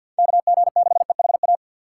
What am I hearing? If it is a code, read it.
Morse code, heard as a keyed tone, about 51 wpm, 700 Hz. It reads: KG6EHA
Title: USA Call Signs - 6 Characters - 3x Character Spacing - 50wpm